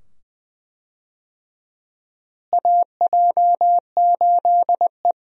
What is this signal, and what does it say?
Morse code, heard as a keyed tone, about 20 wpm, 700 Hz. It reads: AJ8E
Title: USA Call Signs - 4 Characters - Repeated 2x - 20wpm